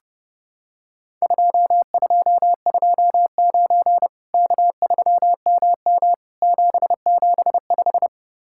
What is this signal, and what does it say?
Morse code, heard as a keyed tone, about 32 wpm, 700 Hz. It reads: 2229 K3MM 775
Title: Worked All Europe DX Contest - QTC Exchange - 30wpm